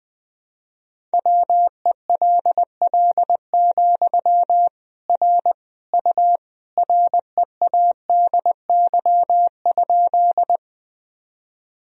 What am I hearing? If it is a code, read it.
Morse code, heard as a keyed tone, about 20 wpm, 700 Hz. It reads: WELL, R U READY?
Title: Sentences from Top 400 Words and Abbreviations 20wpm